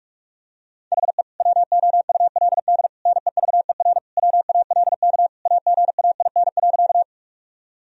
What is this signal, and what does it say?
Morse code, heard as a keyed tone, about 46 wpm, 700 Hz. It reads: HE WOULD NEVER WALK AGAIN.
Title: Sentences from Top 1000 Words - Review Part 1 - 45wpm